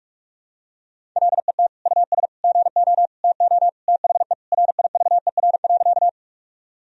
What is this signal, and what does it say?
Morse code, heard as a keyed tone, about 42 wpm, 700 Hz. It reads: LET US GO TO THE RIVER.
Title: Sentences from Top 1000 Words - Review Part 1 - 45wpm